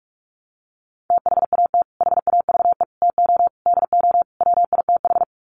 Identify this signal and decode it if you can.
Morse code, heard as a keyed tone, about 44 wpm, 700 Hz. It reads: THAT HAVE TO DO WITH
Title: Sets of 5 Words Encore2 45wpm